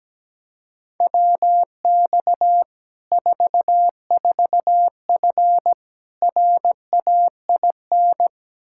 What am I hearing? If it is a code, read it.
Morse code, heard as a keyed tone, about 17 wpm, 700 Hz. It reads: WX 44F RAIN
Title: Practice QSO - Exchange 2 - Info - 17wpm